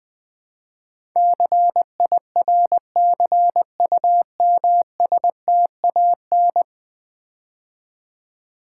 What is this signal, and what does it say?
Morse code, heard as a keyed tone, about 20 wpm, 700 Hz. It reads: CIRCUMSTAN
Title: Build A Word - 5 Letters At A Time - 12-Letter Words - 20wpm